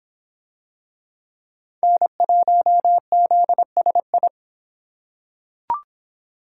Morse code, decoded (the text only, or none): N1ZSI